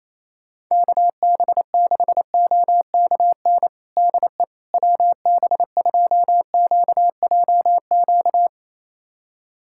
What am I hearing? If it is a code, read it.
Morse code, heard as a keyed tone, about 28 wpm, 700 Hz. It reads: KB6OKN DE WB2QJQ
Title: USA Callsign DE Callsign - 6 Characters - 28wpm